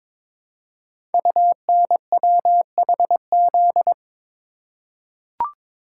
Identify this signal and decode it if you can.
Morse code, heard as a keyed tone, about 22 wpm, 700 Hz. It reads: UNWHZ
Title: Code Groups - 5 Letters - 22wpm